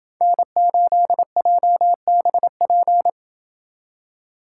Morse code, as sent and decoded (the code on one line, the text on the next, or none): -. ---.. .--- -... .--.
N8JBP